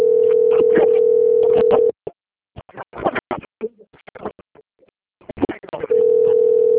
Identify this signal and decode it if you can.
Ringback tone (North America), 440 + 480 Hz.
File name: out- - - - - .11583